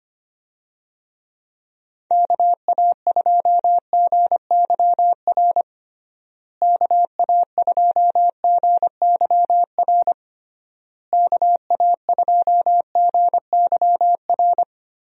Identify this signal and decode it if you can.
Morse code, heard as a keyed tone, about 25 wpm, 700 Hz. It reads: KA2GYR KA2GYR KA2GYR
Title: USA Call Signs - 6 Characters - Repeated 3x - 25wpm